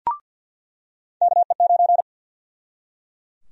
Morse code, heard as a keyed tone, about 47 wpm, 700 Hz. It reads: KE9
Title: USA Call Sign Prefixes - 50wpm